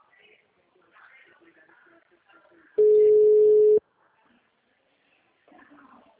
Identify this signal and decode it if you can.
Ringback tone (Europe, 425 Hz), with speech.
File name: out- -2001- - - .31995